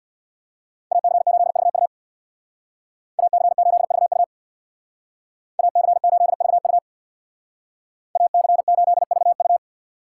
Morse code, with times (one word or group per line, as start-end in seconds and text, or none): 0.9-1.9 AC8VU
3.2-4.2 AC8VU
5.6-6.8 AC8VU
8.2-9.6 AC8VU